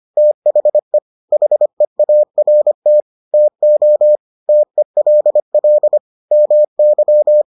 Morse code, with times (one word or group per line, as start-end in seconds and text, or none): 0.2-1.0 THE
1.3-3.0 HEART
3.3-4.2 TO
4.5-6.0 TELL
6.3-7.4 MY